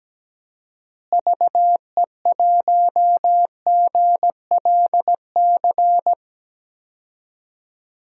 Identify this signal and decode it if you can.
Morse code, heard as a keyed tone, about 17 wpm, 700 Hz. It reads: VE1GLC
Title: DXCC-100 Call Signs - 6 Characters - Repeated 2x - 17wpm